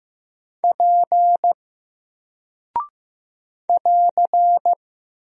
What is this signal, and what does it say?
Morse code, heard as a keyed tone, about 15 wpm, 700 Hz. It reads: P <AR>
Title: Single Character (Letter:Number:Punctuation:Prosign) 15wpm